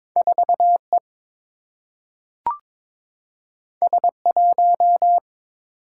Morse code, heard as a keyed tone, about 22 wpm, 700 Hz. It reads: 4 E S 1
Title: Single-Letter Single-Number 22wpm